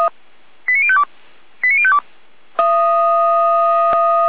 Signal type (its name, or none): five-tone selective call